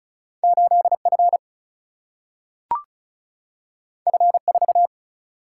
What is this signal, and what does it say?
Morse code, heard as a keyed tone, about 36 wpm, 700 Hz. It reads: 8F F4
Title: Single-Letter Single-Number 35wpm